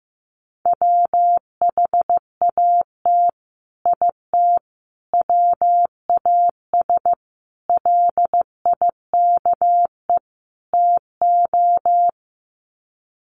Morse code, heard as a keyed tone, about 15 wpm, 700 Hz. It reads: WHAT IT WAS LIKE TO
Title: Sets of 5 Words Encore 15wpm